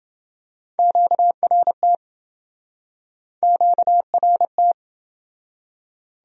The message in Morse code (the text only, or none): QRT QRT